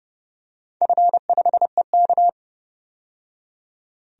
Morse code, heard as a keyed tone, about 30 wpm, 700 Hz. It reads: F5EK
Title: DXCC-100 Call Signs - 4 Characters - 30wpm